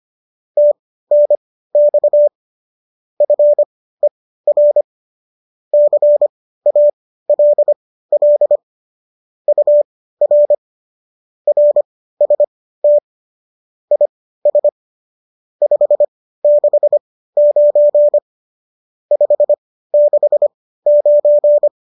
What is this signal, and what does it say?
Morse code, heard as a keyed tone, about 25 wpm, spaced as for 15 wpm, 590 Hz. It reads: TNX FER CALL UR RST IS 569 569
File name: qso203 15